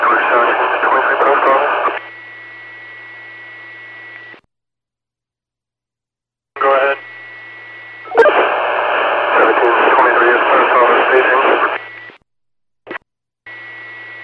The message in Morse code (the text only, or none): none